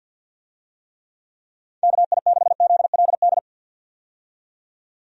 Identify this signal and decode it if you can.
Morse code, heard as a keyed tone, about 52 wpm, 700 Hz. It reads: KI6ZLD
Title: USA Call Signs - 6 Characters - 50wpm